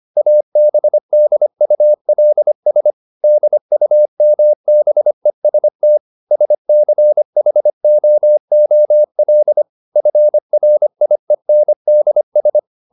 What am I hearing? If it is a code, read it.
Morse code, heard as a keyed tone, about 25 wpm, 600 Hz. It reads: ABDULS DUMBEST SCHOOL FRIENDS
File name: Never 0025wpm